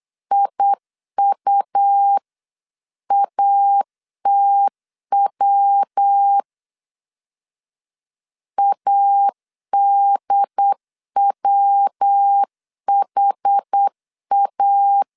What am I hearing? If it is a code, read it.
Morse code, heard as a keyed tone, about 8 wpm, 800 Hz. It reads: IUATW ADWHA